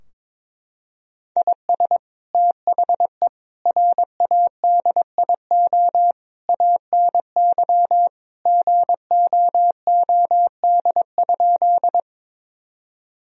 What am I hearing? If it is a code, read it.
Morse code, heard as a keyed tone, about 22 wpm, 700 Hz. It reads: IS THE RADIO ANY GOOD?
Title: Sentences from Top 1000 Words - Review Part 3 - 22wpm